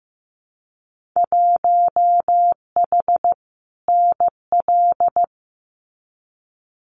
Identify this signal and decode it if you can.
Morse code, heard as a keyed tone, about 15 wpm, 700 Hz. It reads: 1H NL
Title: Winter Field Day Exchanges 15wpm